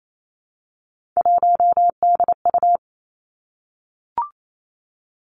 Morse code, heard as a keyed tone, about 28 wpm, 700 Hz. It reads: 1DU
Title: Three Challenging Letter-Number Combinations 28wpm